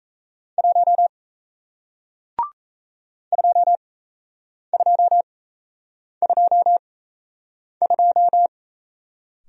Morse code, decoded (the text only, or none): 1 2222